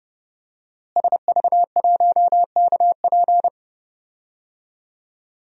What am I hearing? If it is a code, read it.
Morse code, heard as a keyed tone, about 30 wpm, 700 Hz. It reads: SV1KP
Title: DXCC-100 Call Signs - 5 Characters - Repeated 2x - 30wpm